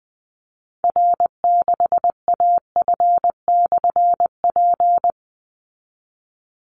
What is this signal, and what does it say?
Morse code, heard as a keyed tone, about 20 wpm, 700 Hz. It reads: R6AF/P
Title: Call Signs Repeated 3x - 20wpm